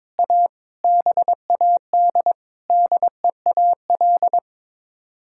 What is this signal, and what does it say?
Morse code, heard as a keyed tone, about 22 wpm, 700 Hz. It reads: A BAD DEAL
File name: practice / 22wpm / Sets of 8 Words - 22wpm